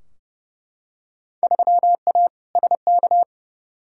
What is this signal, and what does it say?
Morse code, heard as a keyed tone, about 32 wpm, 700 Hz. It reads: 3A SK
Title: Field Day Exchanges Encore - 30wpm